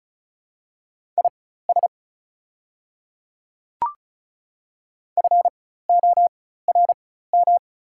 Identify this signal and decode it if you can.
Morse code, heard as a keyed tone, about 35 wpm, spaced as for 16 wpm, 690 Hz. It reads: IS FORM
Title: 4x Character Spacing - Top 200 Words - 35wpm